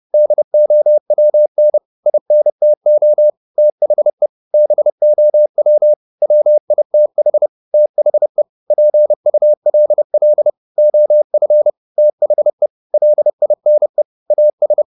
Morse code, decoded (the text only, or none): DOWN INTO THE BOW WITH THE PULL OF THE LINE AS